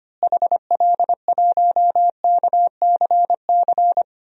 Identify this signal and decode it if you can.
Morse code, heard as a keyed tone, about 25 wpm, 700 Hz. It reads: HL1KCC
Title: DXCC-100 Call Signs - 6 Characters - Repeated 3x - 25wpm